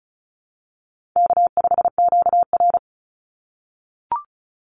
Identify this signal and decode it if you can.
Morse code, heard as a keyed tone, about 34 wpm, 700 Hz. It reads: K5QR
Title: NA SOTA Chaser Callsigns 35wpm